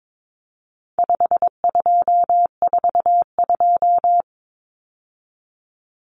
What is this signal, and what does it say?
Morse code, heard as a keyed tone, about 22 wpm, 700 Hz. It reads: 5242
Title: Numbers - 4-Digit 22wpm